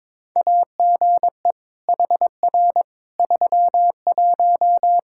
Morse code, heard as a keyed tone, about 21 wpm, 700 Hz. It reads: AGE HR 31
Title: Practice QSO - Exchange 3 - More Info - 22wpm